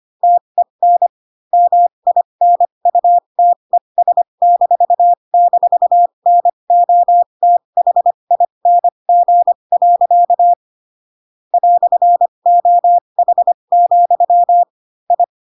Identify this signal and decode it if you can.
Morse code, heard as a keyed tone, about 25 wpm, spaced as for 22 wpm, 700 Hz. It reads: TEN MINUTES--NOTHING. "OH, I